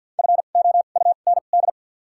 Morse code, heard as a keyed tone, about 52 wpm, spaced as for 36 wpm, 700 Hz. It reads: FOUND
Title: 2x Character Spacing - Top 300 Words - 50wpm